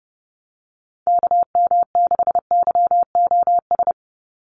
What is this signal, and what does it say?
Morse code, heard as a keyed tone, about 30 wpm, 700 Hz. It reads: KM6YOS
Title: USA Call Signs 30wpm